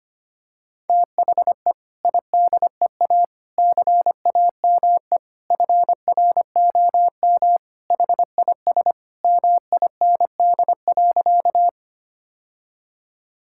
Morse code, heard as a keyed tone, about 25 wpm, 700 Hz. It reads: THE IDEA CAME FROM HIS MIND.
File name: practice / 25wpm / Sentences with 2-Word Phrases - 25wpm